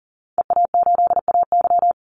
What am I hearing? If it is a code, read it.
Morse code, heard as a keyed tone, about 40 wpm, 700 Hz. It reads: EA8AY
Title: Call Signs - 40wpm